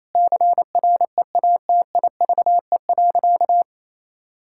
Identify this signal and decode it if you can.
Morse code, heard as a keyed tone, about 28 wpm, 700 Hz. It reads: CREATIVE.
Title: Sentences from Top 1000 Words - Review Part 5 - 28wpm